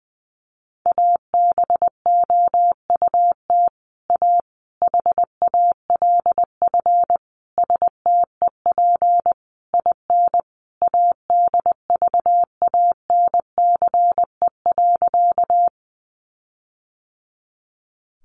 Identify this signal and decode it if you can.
Morse code, heard as a keyed tone, about 20 wpm, 700 Hz. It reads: ABOUT A HALF STEP IN ADVANCE.
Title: Sentences from Top 1400 Words 20wpm